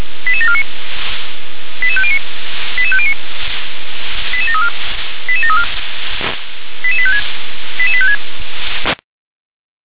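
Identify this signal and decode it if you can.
Five-tone selective call (ZVEI), heard as quick runs of five tones, tones of about 70 ms. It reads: 88749 88509 88509 88734 88734 88856 88856
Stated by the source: ZVEI five-tone selective call received via SDR.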